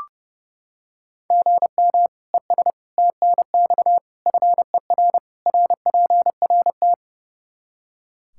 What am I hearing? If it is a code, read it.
Morse code, heard as a keyed tone, about 32 wpm, 700 Hz. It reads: GM ES TNX FER RPRT
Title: K1USN SST QSO Practice 30wpm